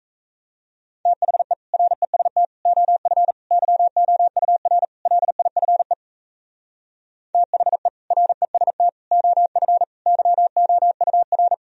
Morse code, heard as a keyed tone, about 36 wpm, 700 Hz. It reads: THE REST OF YOUR LIFE THE REST OF YOUR